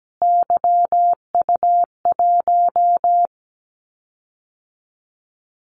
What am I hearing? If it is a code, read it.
Morse code, heard as a keyed tone, about 17 wpm, 700 Hz. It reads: YU1
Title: Callsign Prefixes for Top 25 DXCC - Repeated 3x - 17wpm